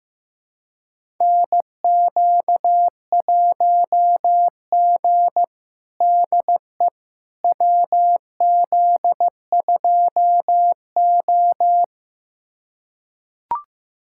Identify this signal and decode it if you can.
Morse code, heard as a keyed tone, about 15 wpm, 700 Hz. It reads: NQ1G DE WZ2O
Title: USA Callsign DE Callsign - 4 Characters - 15wpm